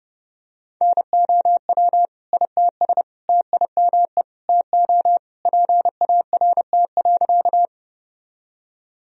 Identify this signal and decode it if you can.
Morse code, heard as a keyed tone, about 30 wpm, 700 Hz. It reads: NOW ITS TIME TO PART.
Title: Sentences from Top 1000 Words - Review Part 1 - 30wpm